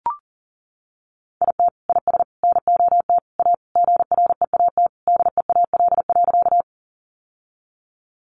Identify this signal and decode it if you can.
Morse code, heard as a keyed tone, about 40 wpm, 700 Hz. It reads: IT IS NOT A GREAT DEAL.
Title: Sentences from Top 800 Words 40wpm